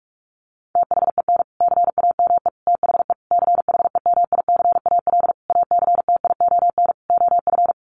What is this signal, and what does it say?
Morse code, heard as a keyed tone, about 44 wpm, 700 Hz. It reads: THEN CAME THE CHEMICAL ACTION OF
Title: Sentences from Top 1200 Words 45wpm